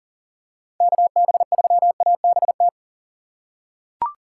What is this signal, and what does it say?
Morse code, heard as a keyed tone, about 40 wpm, 700 Hz. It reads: KB3ABT
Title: USA Call Signs - 4x Character Spacing - 40wpm